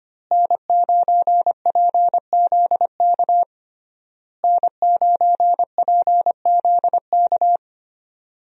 Morse code, heard as a keyed tone, about 25 wpm, 700 Hz. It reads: N9PZK N9PZK
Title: USA Call Signs - 5 Characters - Repeated 3x - 25wpm